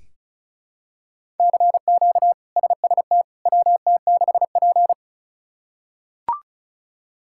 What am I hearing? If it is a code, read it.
Morse code, heard as a keyed tone, about 36 wpm, 700 Hz. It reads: CQ SST WT6P
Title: K1USN SST QSO Practice - 35wpm